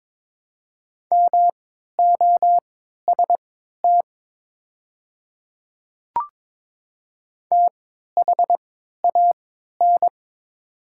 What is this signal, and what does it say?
Morse code, heard as a keyed tone, about 22 wpm, spaced as for 12 wpm, 690 Hz. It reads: MOST THAN
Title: 3x Character Spacing - Top 100 Words - 22wpm